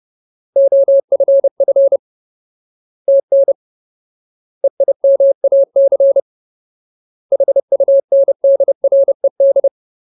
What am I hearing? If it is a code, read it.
Morse code, heard as a keyed tone, about 30 wpm, 550 Hz. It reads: OFF TN EIMAC HUNDRED
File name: Ultimate Word List / 30WPM Ultimate